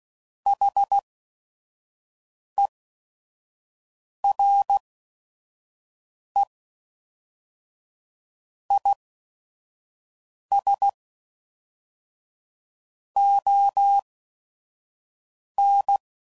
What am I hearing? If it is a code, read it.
Morse code, heard as a keyed tone, about 16 wpm, spaced as for 5 wpm, 800 Hz. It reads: HERE IS ON
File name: CW Code Groups @5WPM / Technical Subjects / OR Gate